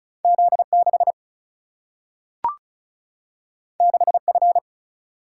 Morse code, heard as a keyed tone, about 36 wpm, 700 Hz. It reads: Z 6 6 F
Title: Single-Letter Single-Number 35wpm